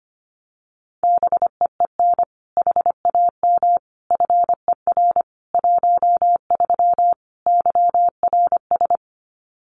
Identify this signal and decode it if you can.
Morse code, heard as a keyed tone, about 25 wpm, 700 Hz. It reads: BEEN HAM FER 13 YRS
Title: Common QSO Phrases 25wpm